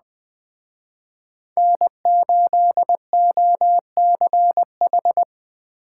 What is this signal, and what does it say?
Morse code, heard as a keyed tone, about 20 wpm, 700 Hz. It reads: N8OCH
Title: Speed Racing - USA 1x3 Call Signs 17wpm